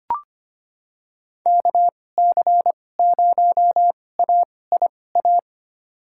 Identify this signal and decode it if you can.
Morse code, heard as a keyed tone, about 25 wpm, spaced as for 18 wpm, 690 Hz. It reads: KC0AIA
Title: USA Call Signs - 6 Characters - 2x Character Spacing - 25wpm